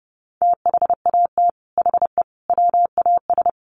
Morse code, heard as a keyed tone, about 30 wpm, 700 Hz. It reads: THAT HE WAS
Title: Sets of 5 Words 30wpm